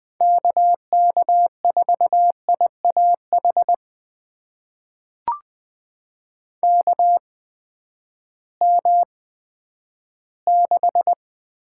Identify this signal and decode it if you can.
Morse code, heard as a keyed tone, about 20 wpm, spaced as for 11 wpm, 700 Hz. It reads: KK4IAH KM6
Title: USA Call Signs - 6 Characters - 8x Character Spacing - 20wpm